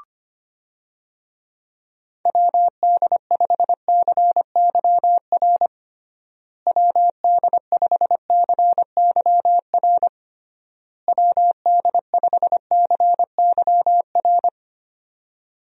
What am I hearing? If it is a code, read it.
Morse code, heard as a keyed tone, about 25 wpm, 700 Hz. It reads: WD5CYR WD5CYR WD5CYR
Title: USA Call Signs - 6 Characters - Repeated 3x - 25wpm